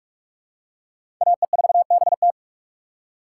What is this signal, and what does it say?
Morse code, heard as a keyed tone, about 43 wpm, 700 Hz. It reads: AE4BT
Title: USA Call Signs - 5 Characters - 3x Character Spacing - 45wpm